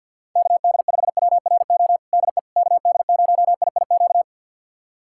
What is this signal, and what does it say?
Morse code, heard as a keyed tone, about 51 wpm, 700 Hz. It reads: KD5WRO DE KD0IIQ
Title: USA Callsign DE Callsign - 6 Characters - 50wpm